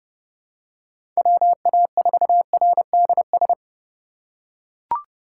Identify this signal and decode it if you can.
Morse code, heard as a keyed tone, about 31 wpm, 700 Hz. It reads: WA4RDS
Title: USA Call Signs - 6 Characters - 2x Character Spacing - 30wpm